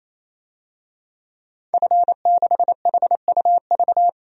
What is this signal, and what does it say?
Morse code, heard as a keyed tone, about 27 wpm, 700 Hz. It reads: F6HUV
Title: DXCC-100 Call Signs - 5 Characters - 28wpm